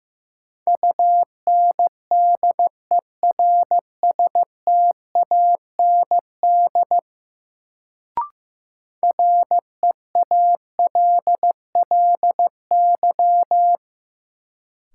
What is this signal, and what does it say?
Morse code, heard as a keyed tone, about 15 wpm, 700 Hz. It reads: UNDERSTAND REALLY
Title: Sentences from Top 600 Words - New Word Drill - 15wpm